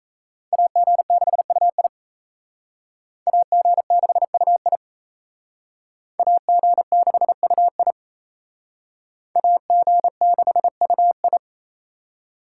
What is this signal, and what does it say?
Morse code, heard as a keyed tone, about 35 wpm, 700 Hz. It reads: AG6UI AG6UI AG6UI AG6UI